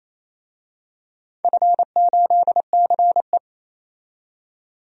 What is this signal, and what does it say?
Morse code, heard as a keyed tone, about 28 wpm, 700 Hz. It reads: F8CE
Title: DXCC-100 Call Signs - 4 Characters - 28wpm